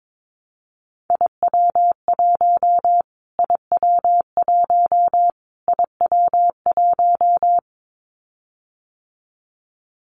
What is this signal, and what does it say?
Morse code, heard as a keyed tone, about 22 wpm, 700 Hz. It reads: IW1 IW1 IW1
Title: Callsign Prefixes for Top 25 DXCC - Repeated 3x - 22wpm